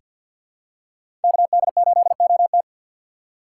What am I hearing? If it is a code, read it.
Morse code, heard as a keyed tone, about 47 wpm, 700 Hz. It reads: KD8OT
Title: USA Call Signs - 5 Characters - 2x Character Spacing - 50wpm